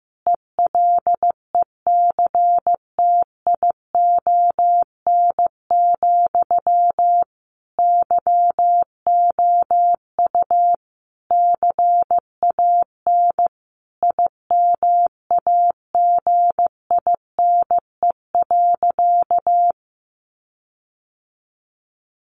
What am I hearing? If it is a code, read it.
Morse code, heard as a keyed tone, about 15 wpm, 700 Hz. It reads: ELECTION, YOU CAN IMAGINE.
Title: Sentences from Top 1500 Words 15wpm